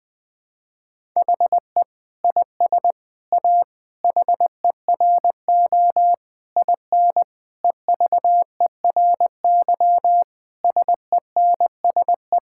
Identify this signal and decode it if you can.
Morse code, heard as a keyed tone, about 20 wpm, 700 Hz. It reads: HE IS A HERO IN EVERY SENSE
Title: Sentences with 6-Word Phrases - Spoken Beginning - 20wpm